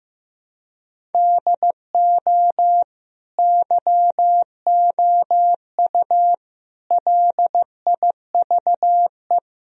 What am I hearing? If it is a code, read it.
Morse code, heard as a keyed tone, about 15 wpm, 700 Hz. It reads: DO YOU LIVE
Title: Sentences from Top 1000 Words - Review Part 1 - 15wpm